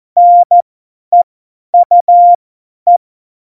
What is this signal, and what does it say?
Morse code, heard as a keyed tone, about 14 wpm, spaced as for 10 wpm, 700 Hz. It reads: NEUE
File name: Morsekurs / Witze / Witz 01. Dezember 2020, 50 F70